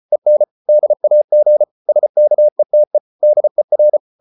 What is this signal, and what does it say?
Morse code, heard as a keyed tone, about 34 wpm, 600 Hz. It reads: EN DAG SKETE DER
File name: Ferie QRQ 34wpm